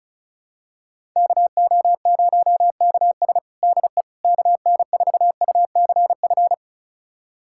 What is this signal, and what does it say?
Morse code, heard as a keyed tone, about 36 wpm, 700 Hz. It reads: KO0KS DE KN4UCF
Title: USA Callsign DE Callsign 35wpm